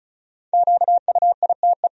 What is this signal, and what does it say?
Morse code, heard as a keyed tone, about 36 wpm, 700 Hz. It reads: QUITE
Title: Sentences from Top 1000 Words - Review Part 2 - 35wpm